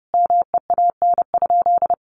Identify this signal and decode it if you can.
Morse code, heard as a keyed tone, about 30 wpm, 700 Hz. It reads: MEAN?
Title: Sentences from Top 200 Words 30wpm